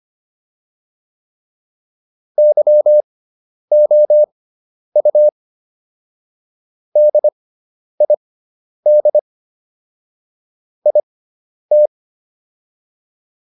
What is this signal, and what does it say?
Morse code, heard as a keyed tone, about 25 wpm, spaced as for 10 wpm, 600 Hz. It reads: YOU DID IT